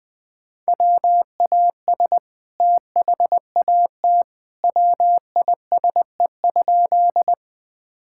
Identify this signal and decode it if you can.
Morse code, heard as a keyed tone, about 20 wpm, 700 Hz. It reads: WAS THAT WISE?
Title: Sentences from Top 1600 Words 20wpm